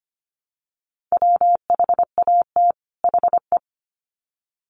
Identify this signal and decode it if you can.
Morse code, heard as a keyed tone, about 25 wpm, 700 Hz. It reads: WHAT HE
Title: Sets of 2 Words 25wpm